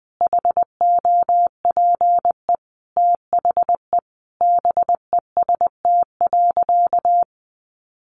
Morse code, heard as a keyed tone, about 20 wpm, 700 Hz. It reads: HOPE THE BEST.
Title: Sentences from Top 1000 Words 20wpm